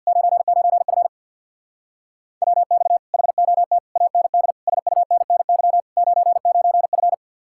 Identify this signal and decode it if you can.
Morse code, heard as a keyed tone, about 64 wpm, 700 Hz. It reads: 99F WX HOT AND SUNNY 99F